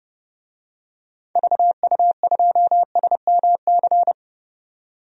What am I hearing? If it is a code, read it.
Morse code, heard as a keyed tone, about 32 wpm, 700 Hz. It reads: VU2SMC
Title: DXCC-100 Call Signs - 6 Characters - 30wpm